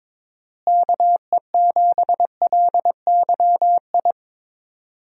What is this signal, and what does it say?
Morse code, heard as a keyed tone, about 22 wpm, 700 Hz. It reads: KE7LYI
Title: USA Call Signs - 6 Characters - Repeated 2x - 22wpm